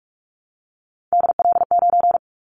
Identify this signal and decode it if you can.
Morse code, heard as a keyed tone, about 44 wpm, 700 Hz. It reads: DL9
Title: Callsign Prefixes for Top 25 DXCC - Repeated 3x - 45wpm